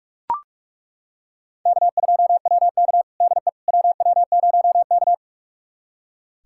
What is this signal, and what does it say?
Morse code, heard as a keyed tone, about 41 wpm, 700 Hz. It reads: K2WK DE WW0K
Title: USA Callsign DE Callsign - 4 Characters - 45wpm